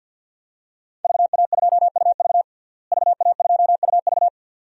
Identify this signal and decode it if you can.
Morse code, heard as a keyed tone, about 52 wpm, 700 Hz. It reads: VA2UV VA2UV
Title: DXCC-100 Call Signs - 5 Characters - Repeated 2x - 50wpm